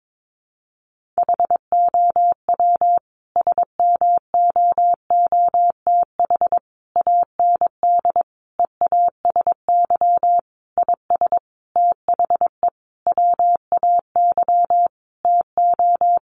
Morse code, heard as a keyed tone, about 22 wpm, 700 Hz. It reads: HOW SMOOTH AND EASY IS THE WAY TO
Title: Sentences from Top 1400 Words 22wpm